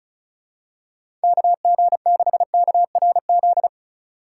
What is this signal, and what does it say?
Morse code, heard as a keyed tone, about 36 wpm, 700 Hz. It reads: KG6KRZ
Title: USA Call Signs - 6 Characters - 4x Character Spacing - 35wpm